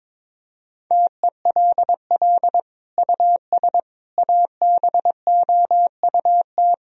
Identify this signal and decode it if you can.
Morse code, heard as a keyed tone, about 22 wpm, 700 Hz. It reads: TELL US ABOUT
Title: Sentences from Top 1000 Words - Review Part 4 - 22wpm